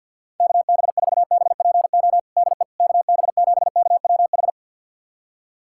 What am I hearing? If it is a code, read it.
Morse code, heard as a keyed tone, about 53 wpm, 700 Hz. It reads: KB4BPO DE KB7KWH
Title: USA Callsign DE Callsign - 6 Characters - 50wpm